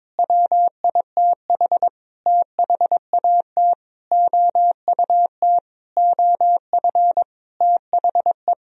Morse code, heard as a keyed tone, about 22 wpm, 700 Hz. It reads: WITH THAT OUT OF THE
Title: Sentences from Top 1000 Words - Review Part 4 - 22wpm